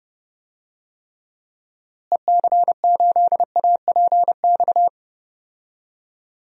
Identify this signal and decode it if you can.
Morse code, heard as a keyed tone, about 30 wpm, 700 Hz. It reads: EC8APX
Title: DXCC-100 Call Signs - 6 Characters - Repeated 3x - 30wpm